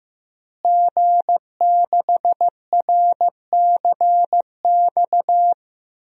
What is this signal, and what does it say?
Morse code, heard as a keyed tone, about 15 wpm, 700 Hz. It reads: G6RCX
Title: DXCC-100 Call Signs - 5 Characters - Repeated 3x - 15wpm